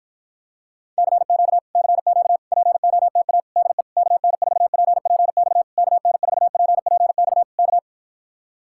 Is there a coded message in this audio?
Yes: Morse code from 1.0 to 7.8 s, heard as a keyed tone, about 54 wpm, 700 Hz. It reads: CQ CQ POTA DE KN4PPX KN4PPX K